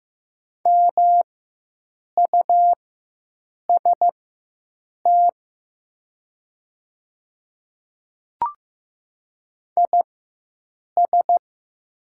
Morse code, heard as a keyed tone, about 15 wpm, spaced as for 7 wpm, 700 Hz. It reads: MUST IS
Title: 4x Character Spacing - Top 300 Words - 15wpm